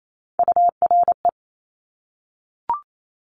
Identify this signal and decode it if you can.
Morse code, heard as a keyed tone, about 28 wpm, 700 Hz. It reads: URE
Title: Top Three Letter Combinations in Words 28wpm